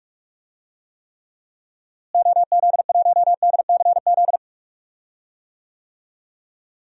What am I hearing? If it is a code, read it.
Morse code, heard as a keyed tone, about 44 wpm, 700 Hz. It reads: OZ1DCZ
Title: DXCC-100 Call Signs - 6 Characters - Repeated 2x - 45wpm